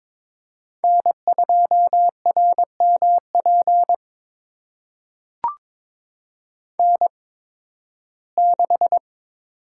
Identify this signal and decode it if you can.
Morse code, heard as a keyed tone, about 22 wpm, spaced as for 14 wpm, 700 Hz. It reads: N2RMP N6
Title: USA Call Signs - 5 Characters - 8x Character Spacing - 22wpm